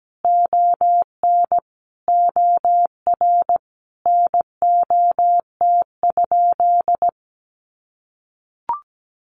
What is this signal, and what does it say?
Morse code, heard as a keyed tone, about 17 wpm, 700 Hz. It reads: ON OR NOT?
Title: Sentences from Top 600 Words 17wpm